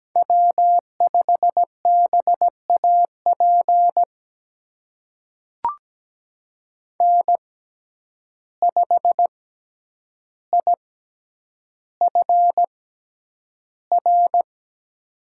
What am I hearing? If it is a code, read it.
Morse code, heard as a keyed tone, about 17 wpm, spaced as for 9 wpm, 700 Hz. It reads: W5BAP N5IFR
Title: USA Call Signs - 5 Characters - 6x Character Spacing - 17wpm